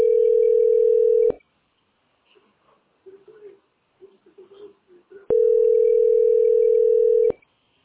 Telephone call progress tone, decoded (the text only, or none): ringback tone